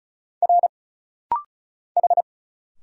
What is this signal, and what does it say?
Morse code, heard as a keyed tone, about 34 wpm, 700 Hz. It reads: R H
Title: Single Letter-Number Alt - Rapid-Fire 35wpm